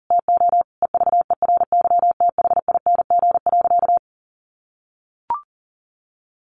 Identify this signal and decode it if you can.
Morse code, heard as a keyed tone, about 40 wpm, 700 Hz. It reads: TO EVERYTHING.
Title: Sentences from Top 700 Words 40wpm